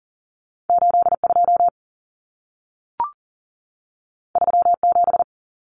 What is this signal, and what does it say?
Morse code, heard as a keyed tone, about 40 wpm, 700 Hz. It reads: 82 37
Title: Numbers - 2-Digit 40wpm